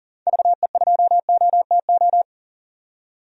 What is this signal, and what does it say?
Morse code, heard as a keyed tone, about 40 wpm, 700 Hz. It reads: VE2OTO
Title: DXCC-100 Call Signs - 6 Characters - 40wpm